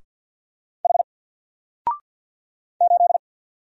Morse code, heard as a keyed tone, about 48 wpm, 700 Hz. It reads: H 8
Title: Single Character (Letter:Number:Punctuation:Prosign) Alt 50wpm